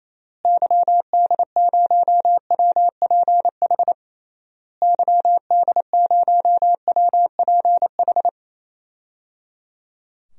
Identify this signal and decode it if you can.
Morse code, heard as a keyed tone, about 28 wpm, 700 Hz. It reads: YD0WPH YD0WPH
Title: DXCC-100 Call Signs - 6 Characters - Repeated 3x - 28wpm